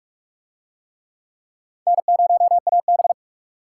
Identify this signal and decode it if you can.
Morse code, heard as a keyed tone, about 39 wpm, 700 Hz. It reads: N0AB